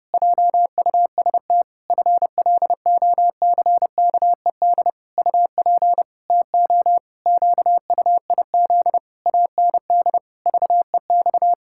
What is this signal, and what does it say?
Morse code, heard as a keyed tone, about 32 wpm, 700 Hz. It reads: JUST FLOCKED UP TO QUIZ AND VEX
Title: Pangram Challenge v2 - 30wpm